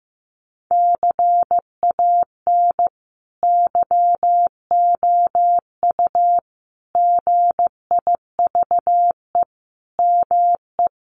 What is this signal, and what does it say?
Morse code, heard as a keyed tone, about 15 wpm, 700 Hz. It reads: CAN YOU GIVE ME
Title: Sentences from Top 1500 Words 15wpm